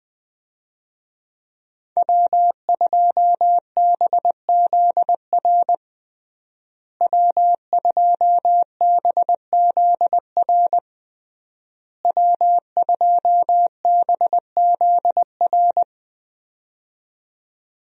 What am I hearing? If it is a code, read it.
Morse code, heard as a keyed tone, about 20 wpm, 700 Hz. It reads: W2BZR W2BZR W2BZR
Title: USA Call Signs - 5 Characters - Repeated 3x - 20wpm